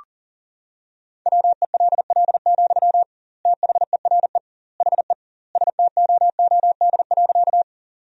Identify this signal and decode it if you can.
Morse code, heard as a keyed tone, about 40 wpm, 700 Hz. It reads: WELL, THERE HE STOOD.
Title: 2x Word-Spacing - Sentences from Top 500 Words - 40wpm